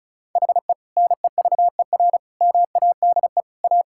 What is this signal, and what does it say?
Morse code, heard as a keyed tone, about 34 wpm, 700 Hz. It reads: HE NEVER MADE A
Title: Sentences from Top 1600 Words 35wpm